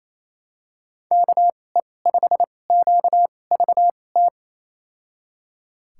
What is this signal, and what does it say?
Morse code, heard as a keyed tone, about 28 wpm, spaced as for 20 wpm, 700 Hz. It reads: KE5QVT
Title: USA Call Signs - 6 Characters - 2x Character Spacing - 28wpm